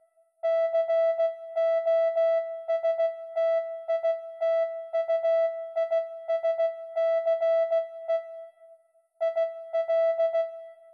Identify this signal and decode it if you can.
Morse code, heard as a keyed tone, about 16 wpm, 660 Hz. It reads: COSTITUISCE IL